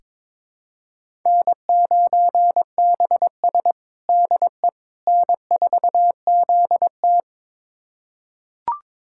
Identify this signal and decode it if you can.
Morse code, heard as a keyed tone, about 22 wpm, 700 Hz. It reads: N9BS DE N4ZT
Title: USA Callsign DE Callsign - 4 Characters - 22wpm